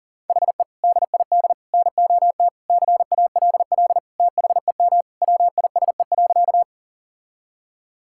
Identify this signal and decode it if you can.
Morse code, heard as a keyed tone, about 40 wpm, 700 Hz. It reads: HE DID NOT CALL THEM WISE.
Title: Sentences from Top 1600 Words 40wpm